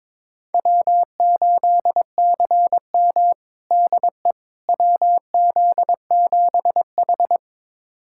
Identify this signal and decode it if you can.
Morse code, heard as a keyed tone, about 22 wpm, 700 Hz. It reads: W8CM DE WZ7H
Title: USA Callsign DE Callsign - 4 Characters - 22wpm